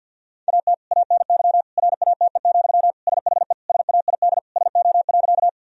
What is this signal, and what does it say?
Morse code, heard as a keyed tone, about 50 wpm, 700 Hz. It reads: AT ANY RATE, SHE SAID SO.
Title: Sentences from Top 1000 Words - Review Part 4 - 50wpm